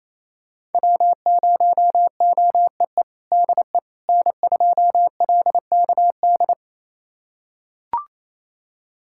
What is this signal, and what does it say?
Morse code, heard as a keyed tone, about 28 wpm, 700 Hz. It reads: W0OEE DE N2LKD
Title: USA Callsign DE Callsign - 5 Characters - 28wpm